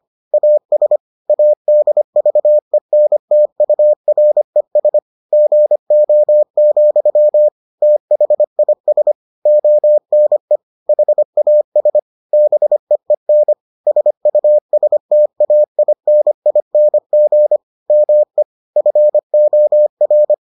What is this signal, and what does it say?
Morse code, heard as a keyed tone, about 25 wpm, 600 Hz. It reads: AS ADVENTURES GO, THIS ONE HAS BEEN SUSTAINING ME FOR